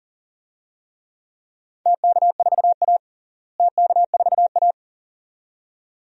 Morse code, heard as a keyed tone, about 40 wpm, 700 Hz. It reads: TK4A TK4A
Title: DXCC-100 Call Signs - 4 Characters - Repeated 2x - 40wpm